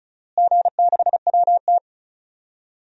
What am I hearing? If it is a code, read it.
Morse code, heard as a keyed tone, about 36 wpm, 700 Hz. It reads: G6WT
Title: DXCC-100 Call Signs - 4 Characters - Repeated 2x - 35wpm